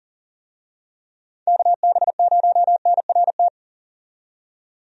Morse code, heard as a keyed tone, about 40 wpm, 700 Hz. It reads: KB0NRT
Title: USA Call Signs - 6 Characters - 40wpm